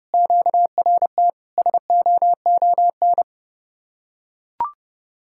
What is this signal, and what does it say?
Morse code, heard as a keyed tone, about 30 wpm, 700 Hz. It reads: QRT SOON